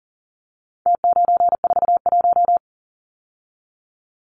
Morse code, heard as a keyed tone, about 38 wpm, 700 Hz. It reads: T941
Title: Numbers - Contest Style - 4-Digit 40wpm